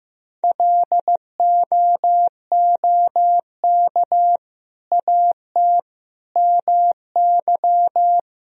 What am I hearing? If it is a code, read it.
Morse code, heard as a keyed tone, about 15 wpm, 700 Hz. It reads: LOOK AT MY
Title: Sentences from Top 1000 Words - Review Part 3 - 15wpm